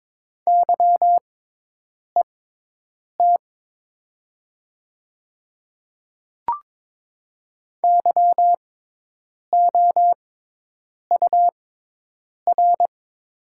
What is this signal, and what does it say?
Morse code, heard as a keyed tone, about 21 wpm, spaced as for 8 wpm, 700 Hz. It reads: YET YOUR
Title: 6x Character Spacing - Top 500 Words - 22wpm